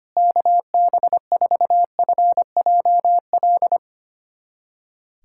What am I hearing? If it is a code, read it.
Morse code, heard as a keyed tone, about 25 wpm, 700 Hz. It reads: KB4FJL
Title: USA Callsign DE Callsign - 6 Characters - 25wpm